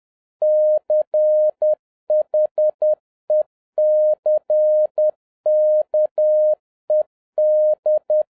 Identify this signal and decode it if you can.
Morse code, heard as a keyed tone, about 10 wpm, 610 Hz. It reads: CHECKED